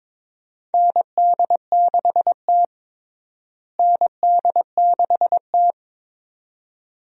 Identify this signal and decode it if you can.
Morse code, heard as a keyed tone, about 22 wpm, 700 Hz. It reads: ND6T ND6T
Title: USA Call Signs - 4 Characters - Repeated 2x - 22wpm